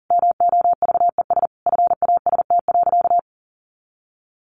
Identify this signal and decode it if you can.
Morse code, heard as a keyed tone, about 40 wpm, 700 Hz. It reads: MOVES FAST.
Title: Easy Phrases 40wpm